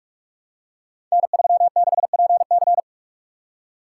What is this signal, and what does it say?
Morse code, heard as a keyed tone, about 44 wpm, 700 Hz. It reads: N36PC
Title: Code Groups - 5 Letter:Numbers - 45wpm